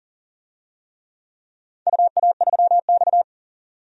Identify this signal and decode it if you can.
Morse code, heard as a keyed tone, about 42 wpm, 700 Hz. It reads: UA3X
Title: DXCC-100 Call Signs - 4 Characters - 40wpm